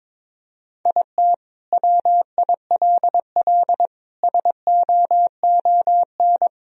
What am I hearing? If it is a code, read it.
Morse code, heard as a keyed tone, about 21 wpm, 700 Hz. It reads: IT WILL SOON
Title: Sentences from Top 1000 Words - Review Part 2 - 22wpm